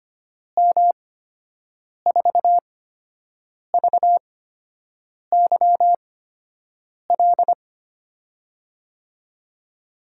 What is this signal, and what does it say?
Morse code, heard as a keyed tone, about 25 wpm, spaced as for 7 wpm, 700 Hz. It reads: M4VYL
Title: USA Call Signs - 6 Characters - 8x Character Spacing - 25wpm